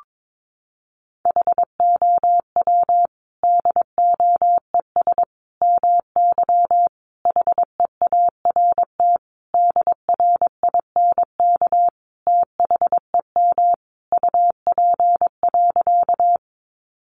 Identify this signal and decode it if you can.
Morse code, heard as a keyed tone, about 22 wpm, 700 Hz. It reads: HOW DOES MY HEART DRINK THEM UP.
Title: Sentences from Top 1500 Words 22wpm